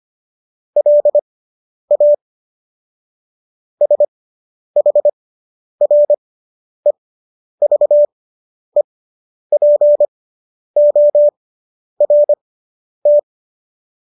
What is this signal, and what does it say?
Morse code, heard as a keyed tone, about 25 wpm, spaced as for 10 wpm, 590 Hz. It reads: LA SHREVEPORT